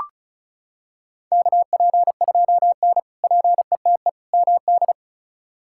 Morse code, heard as a keyed tone, about 36 wpm, 700 Hz. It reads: KP2N PETE MD
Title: K1USN SST QSO Practice - 35wpm